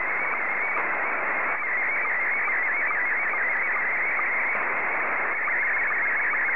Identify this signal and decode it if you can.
SSTV (slow-scan television), Scottie 1.